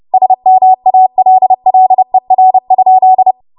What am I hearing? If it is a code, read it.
Morse code, heard as a keyed tone, about 30 wpm, 750 Hz. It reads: SMALLER?
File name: Morse / 30WPM